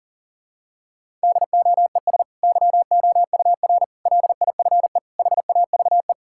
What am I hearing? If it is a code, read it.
Morse code, heard as a keyed tone, about 40 wpm, 700 Hz. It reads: DOES YOUR LIFE HAVE
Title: Sentences from Top 1000 Words - Review Part 6 - 40wpm